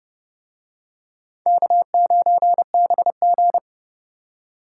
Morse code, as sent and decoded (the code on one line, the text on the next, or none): -.- ----. -... --.
K9BG